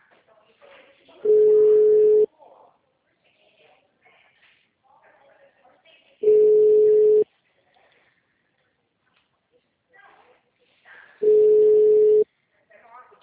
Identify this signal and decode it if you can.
Ringback tone (Europe, 425 Hz), with speech.